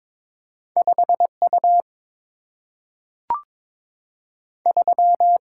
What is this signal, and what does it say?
Morse code, heard as a keyed tone, about 21 wpm, 700 Hz. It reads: 5U 3
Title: Single-Letter Single-Number 22wpm